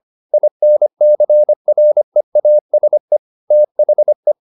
Morse code, heard as a keyed tone, about 25 wpm, 600 Hz. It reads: INCREASE THE
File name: SS315 25